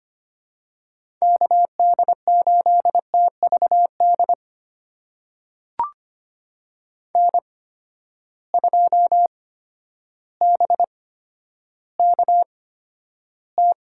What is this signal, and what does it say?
Morse code, heard as a keyed tone, about 25 wpm, spaced as for 11 wpm, 700 Hz. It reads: KD8TVD N2BKT
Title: USA Call Signs - 8x Character Spacing - 25wpm